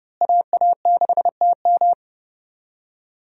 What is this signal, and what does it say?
Morse code, heard as a keyed tone, about 30 wpm, 700 Hz. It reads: AA6TM
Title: USA Call Signs - 5 Characters - Repeated 3x - 30wpm